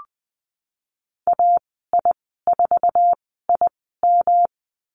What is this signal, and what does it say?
Morse code, heard as a keyed tone, about 20 wpm, spaced as for 15 wpm, 700 Hz. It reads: AI4IM
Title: 161 Lesson 26g - Letter g - Call Signs 20wpm